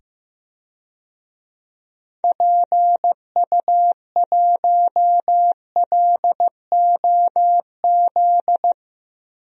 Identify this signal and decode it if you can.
Morse code, heard as a keyed tone, about 15 wpm, 700 Hz. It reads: PU1LOZ
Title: DXCC-100 Call Signs - 6 Characters - 15wpm